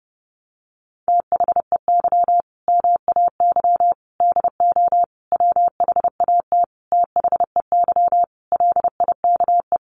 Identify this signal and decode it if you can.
Morse code, heard as a keyed tone, about 30 wpm, 700 Hz. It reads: THEY MAY DO WHAT THEY LIKE
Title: Sentences from Top 1200 Words 30wpm